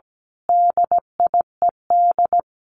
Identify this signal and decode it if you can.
Morse code, heard as a keyed tone, about 17 wpm, 700 Hz. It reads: DIED
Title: A Sound Of Thunder 17wpm